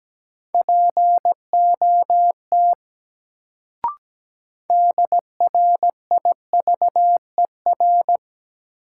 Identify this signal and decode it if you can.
Morse code, heard as a keyed tone, about 17 wpm, 700 Hz. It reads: POT DRIVER
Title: Sentences from Top 1000 Words - New Word Drill - 17wpm